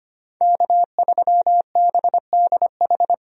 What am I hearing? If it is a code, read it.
Morse code, heard as a keyed tone, about 25 wpm, 700 Hz. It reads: K3BDH
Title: USA Call Signs - 5 Characters - 25wpm